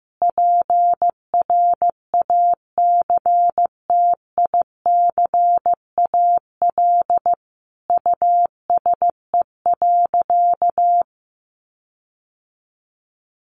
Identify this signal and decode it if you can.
Morse code, heard as a keyed tone, about 15 wpm, 700 Hz. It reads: PRACTICAL USE.
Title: Sentences from Top 1400 Words 15wpm